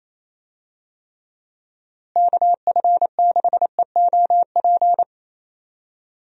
Morse code, heard as a keyed tone, about 28 wpm, 700 Hz. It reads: KF6EOP
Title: USA Call Signs - 6 Characters - Repeated 3x - 28wpm